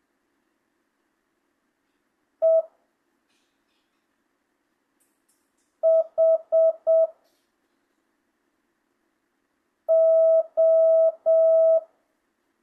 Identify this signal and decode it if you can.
Morse code, heard as a keyed tone, about 7 wpm, 650 Hz. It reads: E H O
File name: jh8cbh / 2025cw